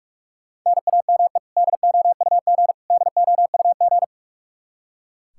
Morse code, heard as a keyed tone, about 43 wpm, 700 Hz. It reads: NAME DOUG DOUG
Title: Practice QSO - Exchange 1 - Beginning - 45wpm